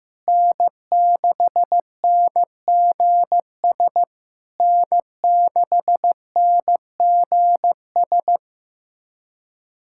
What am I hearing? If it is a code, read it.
Morse code, heard as a keyed tone, about 14 wpm, 700 Hz. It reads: N6NGS N6NGS
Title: SOTA QSO Practice 15wpm